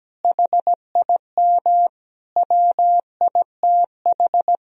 Morse code, heard as a keyed tone, about 17 wpm, 700 Hz. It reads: HIM WITH
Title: Sentences from Top 1000 Words - Review Part 3 - 17wpm